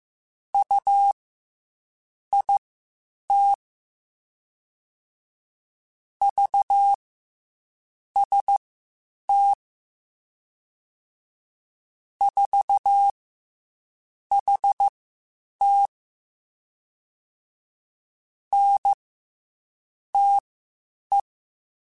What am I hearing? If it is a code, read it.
Morse code, heard as a keyed tone, about 15 wpm, spaced as for 7 wpm, 800 Hz. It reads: UIT VST 4HT NTE
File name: CW Comparisons / CW Comparisons-mp3s / CW Comparisons Broken Characters @13